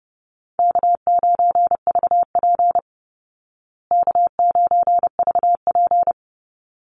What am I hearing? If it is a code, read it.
Morse code, heard as a keyed tone, about 30 wpm, 700 Hz. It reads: K9VP K9VP
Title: Easy Call Signs Repeated 2x - 30wpm